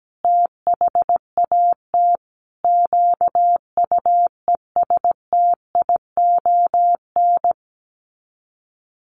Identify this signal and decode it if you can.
Morse code, heard as a keyed tone, about 17 wpm, 700 Hz. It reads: THAT QUESTION
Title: Sets of 5 Words Encore2 17wpm